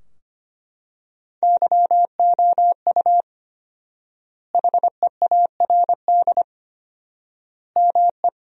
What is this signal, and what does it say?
Morse code, heard as a keyed tone, about 25 wpm, 700 Hz. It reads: YOU HEARD ME
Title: 4x Word-Spacing - Sentences from Top 500 Words - 25wpm